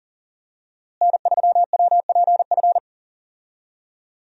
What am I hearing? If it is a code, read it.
Morse code, heard as a keyed tone, about 40 wpm, 700 Hz. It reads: N3WPF
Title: USA Call Signs - 5 Characters - Repeated 3x - 40wpm